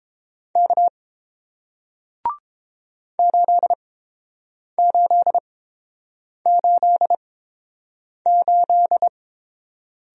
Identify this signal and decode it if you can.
Morse code, heard as a keyed tone, about 30 wpm, spaced as for 7 wpm, 700 Hz. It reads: K 8888